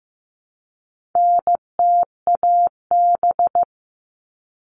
Morse code, heard as a keyed tone, about 15 wpm, 700 Hz. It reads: NTAB
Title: All Cut Numbers - 4-Digit 15wpm